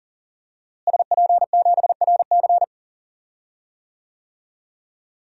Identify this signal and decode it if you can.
Morse code, heard as a keyed tone, about 40 wpm, 700 Hz. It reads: SP7RC
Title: DXCC-100 Call Signs - 5 Characters - Repeated 2x - 40wpm